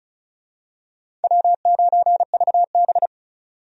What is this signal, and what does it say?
Morse code, heard as a keyed tone, about 36 wpm, 700 Hz. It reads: W9VB
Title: USA Call Signs - 4 Characters - 35wpm